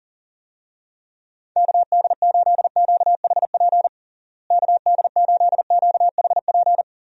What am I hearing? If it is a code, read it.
Morse code, heard as a keyed tone, about 40 wpm, 700 Hz. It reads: KD8QHP KD8QHP
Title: USA Call Signs - 6 Characters - Repeated 2x - 40wpm